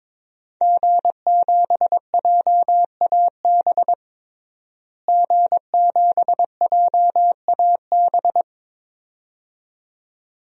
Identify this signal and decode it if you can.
Morse code, heard as a keyed tone, about 21 wpm, 700 Hz. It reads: G7JAB G7JAB
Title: DXCC-100 Call Signs - 5 Characters - Repeated 2x - 22wpm